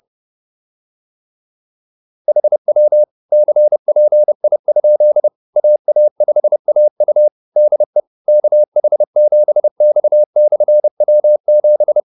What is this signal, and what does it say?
Morse code, heard as a keyed tone, about 30 wpm, 600 Hz. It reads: HW CPI? AA5AU DE KH7X/W7